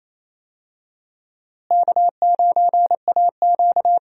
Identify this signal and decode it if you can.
Morse code, heard as a keyed tone, about 28 wpm, 700 Hz. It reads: K9AQ
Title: USA Call Signs - 4 Characters - Repeated 2x - 28wpm